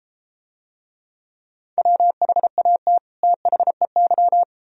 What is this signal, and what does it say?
Morse code, heard as a keyed tone, about 33 wpm, 700 Hz. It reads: WHAT THEY